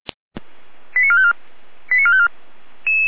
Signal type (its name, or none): five-tone selective call